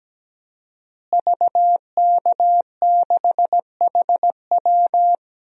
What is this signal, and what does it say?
Morse code, heard as a keyed tone, about 17 wpm, 700 Hz. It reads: VK6HW
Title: DXCC-100 Call Signs - 5 Characters - Repeated 2x - 17wpm